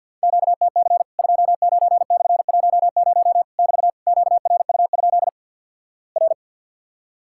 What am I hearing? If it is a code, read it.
Morse code, heard as a keyed tone, about 50 wpm, 700 Hz. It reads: QTC 29/10 <BT> QRU? R
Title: Worked All Europe DX Contest - QTC Exchange with Repeats - 50wpm